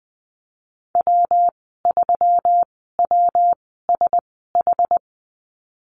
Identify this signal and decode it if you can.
Morse code, heard as a keyed tone, about 20 wpm, spaced as for 15 wpm, 700 Hz. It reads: W3WSH
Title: 140 Lesson 23g - Letter f - Call Signs 20wpm